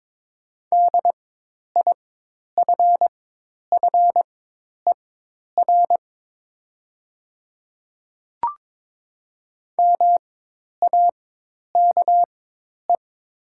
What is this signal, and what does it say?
Morse code, heard as a keyed tone, about 22 wpm, spaced as for 10 wpm, 700 Hz. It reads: DIFFER MAKE
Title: 4x Character Spacing - Top 200 Words - 22wpm